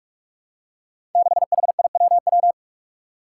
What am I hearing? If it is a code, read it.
Morse code, heard as a keyed tone, about 43 wpm, 700 Hz. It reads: 6HIWW
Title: Five Challenging Letter-Number Combinations 45wpm